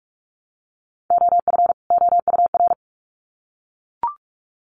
Morse code, heard as a keyed tone, about 44 wpm, 700 Hz. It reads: OF OUR
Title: Sets of 2 Words 45wpm